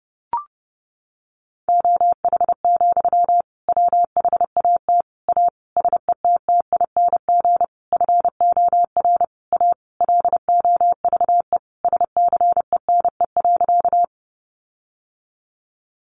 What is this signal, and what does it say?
Morse code, heard as a keyed tone, about 30 wpm, 700 Hz. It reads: OH, WHAT A SETTING FOR A LOVE SCENE.
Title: Sentences from Top 1300 Words 30wpm